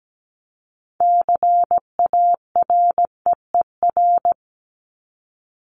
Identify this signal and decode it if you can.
Morse code, heard as a keyed tone, about 17 wpm, 700 Hz. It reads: CAREER
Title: Top 751-1000 Words 17wpm